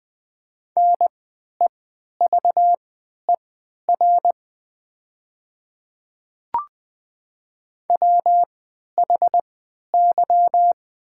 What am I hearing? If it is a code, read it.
Morse code, heard as a keyed tone, about 20 wpm, spaced as for 11 wpm, 700 Hz. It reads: NEVER WHY
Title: 3x Character Spacing - Top 300 Words - 20wpm